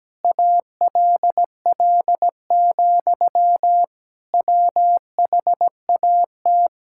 Morse code, heard as a keyed tone, about 17 wpm, 700 Hz. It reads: ALL, WHAT
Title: Sentences from Top 1600 Words 17wpm